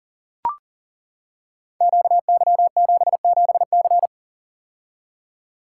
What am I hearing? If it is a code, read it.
Morse code, heard as a keyed tone, about 43 wpm, 700 Hz. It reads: QY77C
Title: Code Groups - 5 Letter:Numbers - 40wpm